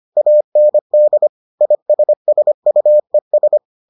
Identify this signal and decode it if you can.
Morse code, heard as a keyed tone, about 25 wpm, 600 Hz. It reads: AND ISSUES